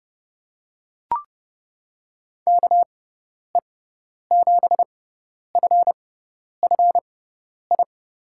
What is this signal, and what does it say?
Morse code, heard as a keyed tone, about 30 wpm, spaced as for 10 wpm, 700 Hz. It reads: KE7FFI
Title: USA Call Signs - 6 Characters - 6x Character Spacing - 30wpm